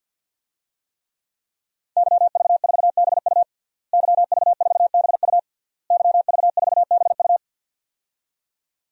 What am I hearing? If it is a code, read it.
Morse code, heard as a keyed tone, about 51 wpm, 700 Hz. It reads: YV4BU YV4BU YV4BU
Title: DXCC-100 Call Signs - 5 Characters - Repeated 3x - 50wpm